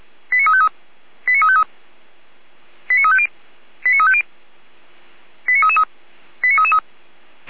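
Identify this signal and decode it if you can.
Five-tone selective call (ZVEI), heard as quick runs of five tones, tones of about 70 ms. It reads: 89353 89353 89350 89350 89333
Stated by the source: ZVEI five-tone selective call 89510 received on SDR.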